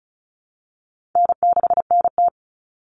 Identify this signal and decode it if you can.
Morse code, heard as a keyed tone, about 34 wpm, 700 Hz. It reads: N6NT
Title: Common Cut Numbers - 4-Digit 35wpm